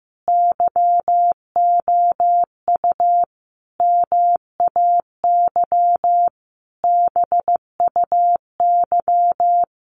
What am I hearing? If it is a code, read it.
Morse code, heard as a keyed tone, about 15 wpm, 700 Hz. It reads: YOU MAY BUY
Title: Sentences from Top 800 Words 15wpm